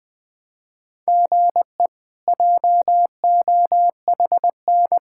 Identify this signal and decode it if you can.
Morse code, heard as a keyed tone, about 21 wpm, 700 Hz. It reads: GE JOHN
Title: K1USN SST QSO Practice - 20wpm